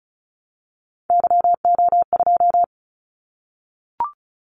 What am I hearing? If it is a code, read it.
Morse code, heard as a keyed tone, about 36 wpm, 700 Hz. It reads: YO2
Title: Callsign Prefixes for Top 25 DXCC - 35wpm